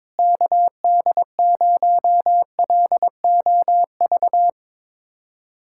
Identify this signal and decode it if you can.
Morse code, heard as a keyed tone, about 22 wpm, 700 Hz. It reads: KD0LOV
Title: USA Callsign DE Callsign 22wpm